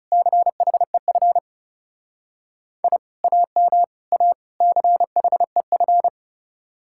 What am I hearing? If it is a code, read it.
Morse code, heard as a keyed tone, about 36 wpm, 700 Hz. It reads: CHEF I AM A CHEF
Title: Speed-Racing Common QSO Phrases Encore - 30wpm